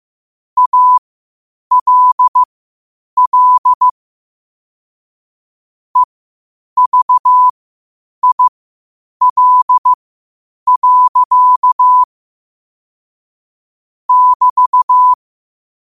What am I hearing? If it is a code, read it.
Morse code, heard as a keyed tone, about 15 wpm, spaced as for 8 wpm, 1000 Hz. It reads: ALL EVIL. <BT>
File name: qotdfiles / 8wpm